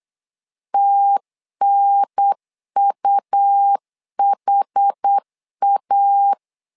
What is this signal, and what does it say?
Morse code, heard as a keyed tone, about 8 wpm, 800 Hz. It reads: TNUHA
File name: LEZIONI / 5 LBVC / 5 LBVC 8wpm 2935